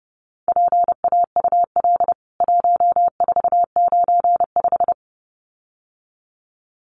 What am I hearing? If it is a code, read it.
Morse code, heard as a keyed tone, about 30 wpm, 700 Hz. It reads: PAUL 1495
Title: CWT Exchange 30wpm